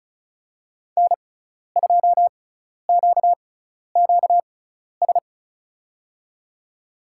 Morse code, heard as a keyed tone, about 36 wpm, spaced as for 12 wpm, 700 Hz. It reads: N2QQS
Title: USA Call Signs - 5 Characters - 6x Character Spacing - 35wpm